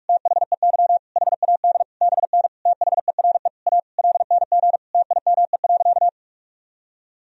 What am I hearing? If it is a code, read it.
Morse code, heard as a keyed tone, about 41 wpm, 700 Hz. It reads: THEY HAD BN THERE A LNG TIME.
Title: Sentences from Top 400 Words and Abbreviations 45wpm